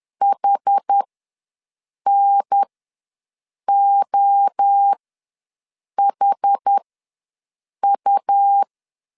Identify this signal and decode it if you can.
Morse code, heard as a keyed tone, about 10 wpm, spaced as for 6 wpm, 800 Hz. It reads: HNOHU
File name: LEZIONI / 6 PKX0 / 6 P ws 10wpm 3751